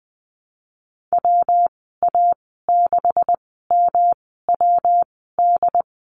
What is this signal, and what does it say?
Morse code, heard as a keyed tone, about 20 wpm, spaced as for 15 wpm, 700 Hz. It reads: WA6MWD
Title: 147 Lesson 24g - Letter y - Call Signs 20wpm